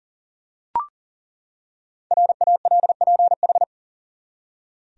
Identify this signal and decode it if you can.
Morse code, heard as a keyed tone, about 38 wpm, 700 Hz. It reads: RALPH
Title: USA Common Names - 40wpm